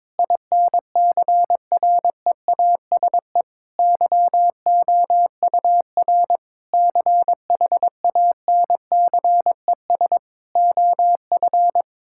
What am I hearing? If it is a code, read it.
Morse code, heard as a keyed tone, about 21 wpm, 700 Hz. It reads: INCREASE YOUR CHANCES OF
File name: practice / 22wpm / Sentences with 8-Word Phrases - 22wpm